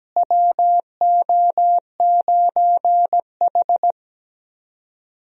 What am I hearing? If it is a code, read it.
Morse code, heard as a keyed tone, about 17 wpm, 700 Hz. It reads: WO9H
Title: USA Call Signs - 4 Characters - Repeated 3x - 17wpm